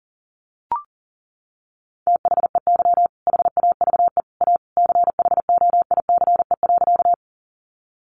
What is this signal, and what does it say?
Morse code, heard as a keyed tone, about 40 wpm, 700 Hz. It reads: THEY HAVE A CHOICE.
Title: Sentences from Top 900 Words 40wpm